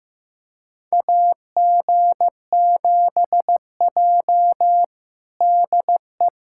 Morse code, heard as a keyed tone, about 15 wpm, 700 Hz. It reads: AG7J DE
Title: USA Callsign DE Callsign - 4 Characters - 15wpm